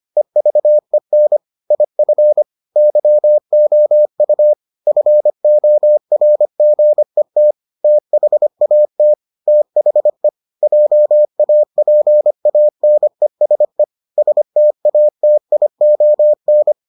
Morse code, heard as a keyed tone, about 25 wpm, 600 Hz. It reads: EVEN IF YOU FORGET THAT THE JAPANESE STATION